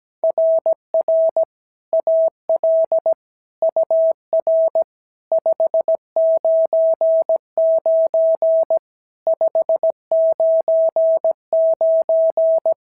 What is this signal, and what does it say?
Morse code, heard as a keyed tone, about 16 wpm, 650 Hz. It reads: RR AL UR 599 599
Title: SOTA QSO Practice 17wpm